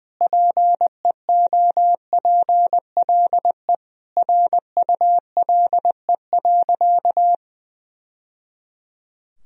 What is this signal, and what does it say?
Morse code, heard as a keyed tone, about 19 wpm, 700 Hz. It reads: PEOPLE RULE.
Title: Sentences from Top 1000 Words - Review Part 2 - 20wpm